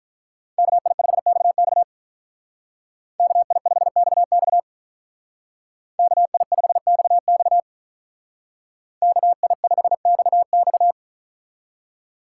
Morse code, heard as a keyed tone, about 46 wpm, 700 Hz. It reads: KI5XX KI5XX KI5XX KI5XX